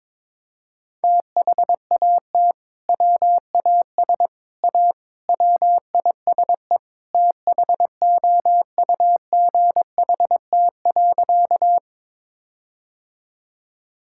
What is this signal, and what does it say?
Morse code, heard as a keyed tone, about 22 wpm, 700 Hz. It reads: THAT WAS A WISE THOUGHT.
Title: Sentences from Top 1600 Words 22wpm